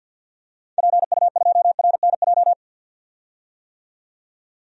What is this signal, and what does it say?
Morse code, heard as a keyed tone, about 50 wpm, 700 Hz. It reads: PU2RNJ
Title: DXCC-100 Call Signs - 6 Characters - 50wpm